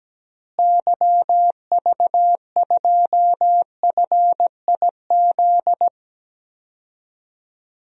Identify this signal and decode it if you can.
Morse code, heard as a keyed tone, about 17 wpm, 700 Hz. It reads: YV2FIZ
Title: DXCC-100 Call Signs - 6 Characters - Repeated 2x - 17wpm